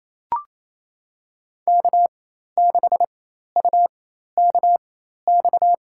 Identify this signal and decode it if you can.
Morse code, heard as a keyed tone, about 28 wpm, spaced as for 13 wpm, 700 Hz. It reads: K6UKX
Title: USA Call Signs - 4x Character Spacing - 28wpm